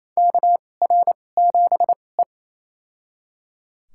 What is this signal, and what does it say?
Morse code, heard as a keyed tone, about 27 wpm, spaced as for 20 wpm, 700 Hz. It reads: KR7E
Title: USA Call Signs - 4 Characters - 2x Character Spacing - 28wpm